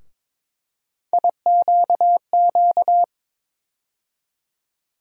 Morse code, heard as a keyed tone, about 22 wpm, 700 Hz. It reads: IQQ
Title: USA Call Sign Postfixes - 3 Letters 22wpm